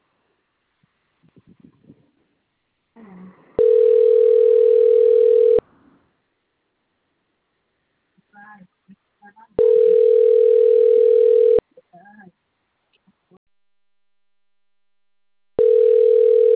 Ringback tone (North America), 440 + 480 Hz.